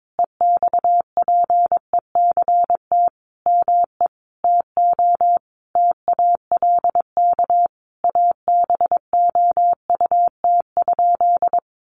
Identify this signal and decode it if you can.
Morse code, heard as a keyed tone, about 22 wpm, 700 Hz. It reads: EXPECT ME TO TALK ABOUT?
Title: Sentences from Top 1100 Words 22wpm